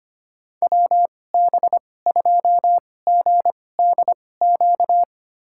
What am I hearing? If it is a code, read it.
Morse code, heard as a keyed tone, about 25 wpm, spaced as for 18 wpm, 700 Hz. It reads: WB2GDQ
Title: USA Call Signs - 6 Characters - 2x Character Spacing - 25wpm